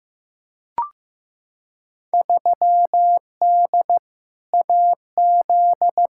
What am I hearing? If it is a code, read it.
Morse code, heard as a keyed tone, about 15 wpm, 700 Hz. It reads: 3D AZ
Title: Field Day Exchanges - 15wpm